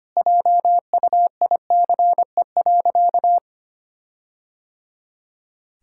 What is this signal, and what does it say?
Morse code, heard as a keyed tone, about 25 wpm, 700 Hz. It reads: JUICE.
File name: practice / 25wpm / All-Age Appropriate Jokes - 25wpm